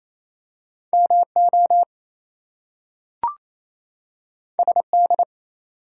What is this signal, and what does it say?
Morse code, heard as a keyed tone, about 28 wpm, 700 Hz. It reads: MO SD
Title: US States and Canadian Provinces 28wpm